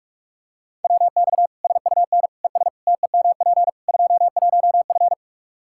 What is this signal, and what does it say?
Morse code, heard as a keyed tone, about 46 wpm, 700 Hz. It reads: WX SUN ES TEMP 21F
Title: Practice QSO - Exchange 2 - Info - 45wpm